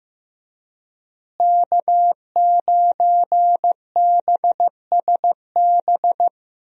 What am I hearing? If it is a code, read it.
Morse code, heard as a keyed tone, about 16 wpm, 700 Hz. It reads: K9BSB
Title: USA Call Signs - 5 Characters - Repeated 3x - 15wpm